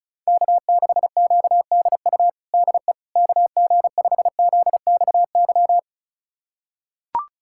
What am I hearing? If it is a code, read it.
Morse code, heard as a keyed tone, about 34 wpm, 690 Hz. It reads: K6QDU DE KG5ZXY
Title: USA Callsign DE Callsign 35wpm